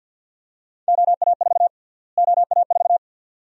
Morse code, heard as a keyed tone, about 52 wpm, 700 Hz. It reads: OA4 OA4
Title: USA Call Sign Prefixes - Repeated 2x - 50wpm